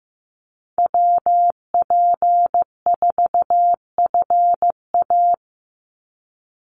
Morse code, heard as a keyed tone, about 15 wpm, 700 Hz. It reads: WP4FA
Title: USA Call Signs 15wpm